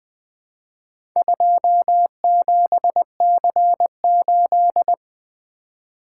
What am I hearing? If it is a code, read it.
Morse code, heard as a keyed tone, about 20 wpm, 700 Hz. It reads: 27C8
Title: Four Letter-Number Combinations - 20wpm